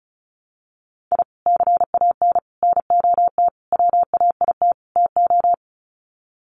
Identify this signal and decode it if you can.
Morse code, heard as a keyed tone, about 34 wpm, 700 Hz. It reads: I CAN NOT WAIT TO
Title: Sets of 5 Words Encore 35wpm